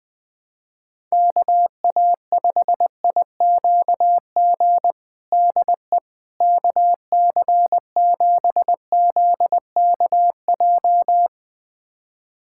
Morse code, heard as a keyed tone, about 20 wpm, 700 Hz. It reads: KA5IQG DE KC7ZKJ
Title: USA Callsign DE Callsign - 6 Characters - 20wpm